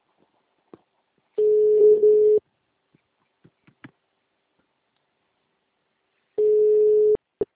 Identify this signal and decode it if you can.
Ringback tone (Europe, 425 Hz).
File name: out- -2003- - - .9857